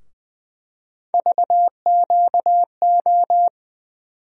Morse code, heard as a keyed tone, about 21 wpm, 700 Hz. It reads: VQO
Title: USA Call Sign Postfixes - 3 Letters 20wpm